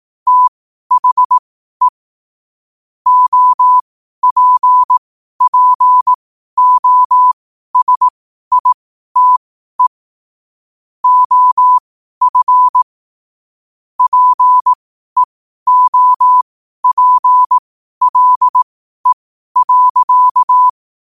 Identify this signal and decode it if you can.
Morse code, heard as a keyed tone, about 18 wpm, spaced as for 12 wpm, 1000 Hz. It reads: THE OPPOSITE OF PEOPLE.